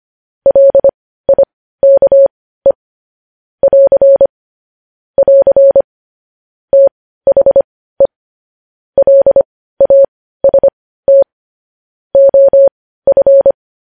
Morse code, heard as a keyed tone, about 25 wpm, spaced as for 15 wpm, 550 Hz. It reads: LIKE <AR> <AR> THE LAST OF
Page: news headlines in morse code at 15 wpm